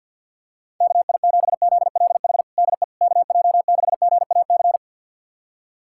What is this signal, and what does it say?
Morse code, heard as a keyed tone, about 48 wpm, 700 Hz. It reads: KI7ZLH DE KJ6GAC
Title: USA Callsign DE Callsign - 6 Characters - 50wpm